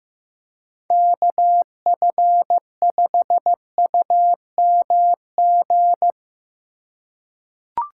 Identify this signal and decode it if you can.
Morse code, heard as a keyed tone, about 15 wpm, 700 Hz. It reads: KF5UMG
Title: USA Call Signs - 6 Characters - 2x Character Spacing - 15wpm